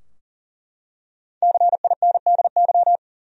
Morse code, heard as a keyed tone, about 40 wpm, 700 Hz. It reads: CINDY
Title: USA Common Names - Max 5 Letters - 40wpm